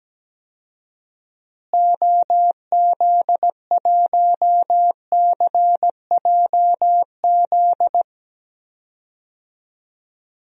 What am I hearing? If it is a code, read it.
Morse code, heard as a keyed tone, about 17 wpm, 700 Hz. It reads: OZ1CJZ
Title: DXCC-100 Call Signs - 6 Characters - Repeated 3x - 17wpm